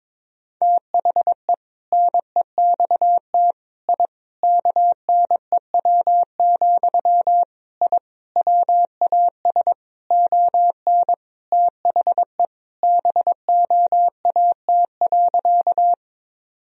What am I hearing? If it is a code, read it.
Morse code, heard as a keyed tone, about 21 wpm, 700 Hz. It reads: THE NEXT I KNEW, I WAS ON THE BOAT.
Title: Sentences from Top 1000 Words - Review Part 3 - 22wpm